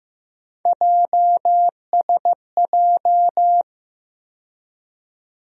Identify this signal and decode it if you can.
Morse code, heard as a keyed tone, about 15 wpm, 700 Hz. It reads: JSJ
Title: USA Call Sign Postfixes - 3 Letters 15wpm